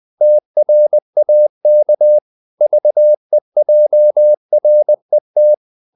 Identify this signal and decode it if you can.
Morse code, heard as a keyed tone, about 20 wpm, 600 Hz. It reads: TRAK VEJRET
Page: Ferie 19 wpm